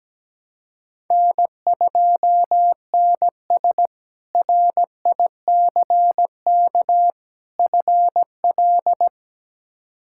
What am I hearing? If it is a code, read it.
Morse code, heard as a keyed tone, about 17 wpm, 700 Hz. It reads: N2NS RICK FL
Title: K1USN SST QSO Practice - 17wpm